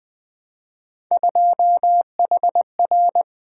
Morse code, heard as a keyed tone, about 20 wpm, 700 Hz. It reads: 2HR
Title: Three Letter-Number Combinations - 20wpm